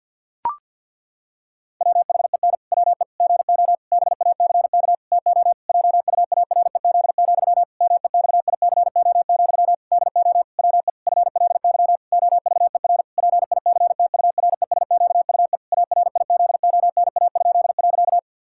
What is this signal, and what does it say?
Morse code, heard as a keyed tone, about 50 wpm, 700 Hz. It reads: WHEN WE GO BACK TO JUAREZ, MEXICO, DO WE FLY OVER PICTURESQUE ARIZONA?.
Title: Pangram Challenge v2 - 50wpm